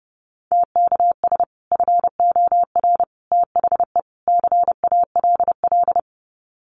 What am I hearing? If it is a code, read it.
Morse code, heard as a keyed tone, about 30 wpm, 700 Hz. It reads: TKS FOR THE CALL
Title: Common QSO Phrases 30wpm